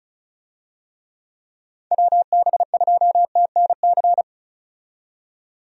Morse code, heard as a keyed tone, about 36 wpm, 700 Hz. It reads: WB2TNC
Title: USA Call Signs - 6 Characters - 35wpm